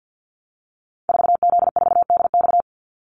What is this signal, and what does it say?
Morse code, heard as a keyed tone, about 50 wpm, 700 Hz. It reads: 4Z4DX
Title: Call Signs Repeated 2x - 50wpm